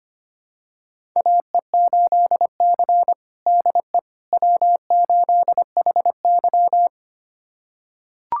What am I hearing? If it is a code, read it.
Morse code, heard as a keyed tone, about 25 wpm, 700 Hz. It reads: AE8C DE W8HY
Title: USA Callsign DE Callsign - 4 Characters - 25wpm